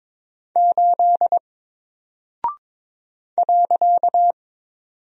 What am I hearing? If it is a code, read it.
Morse code, heard as a keyed tone, about 22 wpm, 700 Hz. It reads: 8 .
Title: Single Character (Letter:Number:Punctuation:Prosign) 22wpm